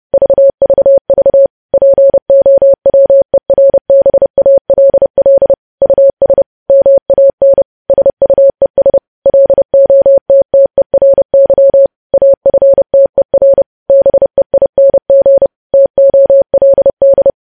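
Morse code, heard as a keyed tone, about 30 wpm, 550 Hz. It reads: VVV POWERBALL US MAN SUES LOTTERY AFTER BEING TOLD
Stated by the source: Morse code news headlines practice, preceded by vvv.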